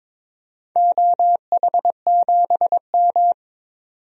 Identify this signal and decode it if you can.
Morse code, heard as a keyed tone, about 22 wpm, 700 Hz. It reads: OH7M
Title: DXCC-100 Call Signs - 4 Characters - 22wpm